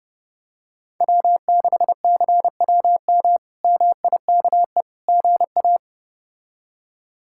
Morse code, heard as a keyed tone, about 31 wpm, 700 Hz. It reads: W6CWM MIKE GA
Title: K1USN SST QSO Practice - 30wpm